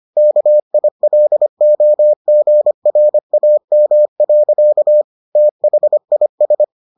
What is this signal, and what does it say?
Morse code, heard as a keyed tone, about 25 wpm, 600 Hz. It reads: KILOGRAM. THIS